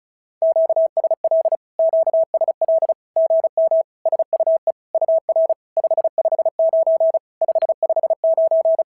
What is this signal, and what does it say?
Morse code, heard as a keyed tone, about 33 wpm, 650 Hz. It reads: QSL QSL GM SUE UR 559 559
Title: SOTA QSO Practice 35wpm